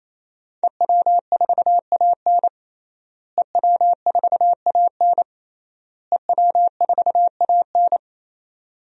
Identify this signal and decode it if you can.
Morse code, heard as a keyed tone, about 28 wpm, 700 Hz. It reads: EW4AN EW4AN EW4AN
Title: DXCC-100 Call Signs - 5 Characters - Repeated 3x - 28wpm